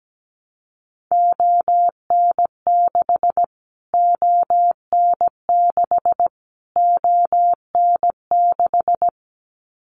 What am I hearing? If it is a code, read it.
Morse code, heard as a keyed tone, about 17 wpm, 700 Hz. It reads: ON6 ON6 ON6
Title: Callsign Prefixes for Top 25 DXCC - Repeated 3x - 17wpm